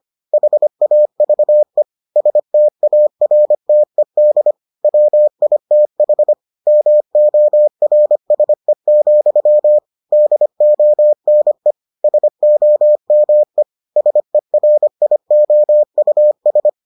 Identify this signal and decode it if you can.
Morse code, heard as a keyed tone, about 25 wpm, 600 Hz. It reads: HAVE STARTED WITH MORSE, DONE SOME SERIOUS